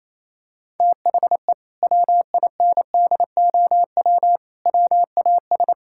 Morse code, heard as a keyed tone, about 28 wpm, 700 Hz. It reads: THE WINDOW WAS
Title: Sentences from Top 1600 Words 28wpm